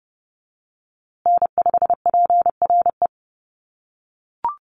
Morse code, heard as a keyed tone, about 30 wpm, 700 Hz. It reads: N5PRE
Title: USA Call Signs Repeated 2x 30wpm